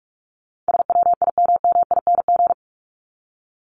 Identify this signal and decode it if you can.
Morse code, heard as a keyed tone, about 44 wpm, 700 Hz. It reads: SWIMMING
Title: Top 1501-1750 Words 45wpm